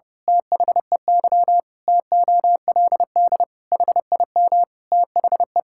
Morse code, heard as a keyed tone, about 28 wpm, 700 Hz. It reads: THEY TOLD HIM THE
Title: Pangram Challenge v2 - 30wpm